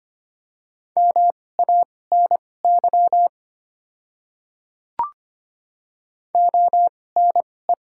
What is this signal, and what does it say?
Morse code, heard as a keyed tone, about 25 wpm, spaced as for 18 wpm, 690 Hz. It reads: MANY ONE
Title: 2x Character Spacing - Top 100 Words - 25wpm